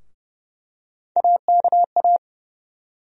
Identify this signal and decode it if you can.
Morse code, heard as a keyed tone, about 30 wpm, 700 Hz. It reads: AKA
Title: USA Call Sign Postfixes - 3 Letters 30wpm